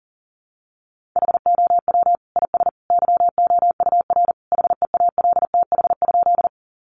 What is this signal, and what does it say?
Morse code, heard as a keyed tone, about 40 wpm, 700 Hz. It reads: HOW IS YOUR HEALTH?
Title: Sentences from Top 600 Words 40wpm